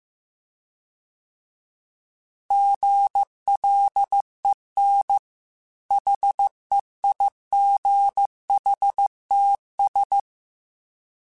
Morse code, heard as a keyed tone, about 15 wpm, 800 Hz. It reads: GLEN HEIGHTS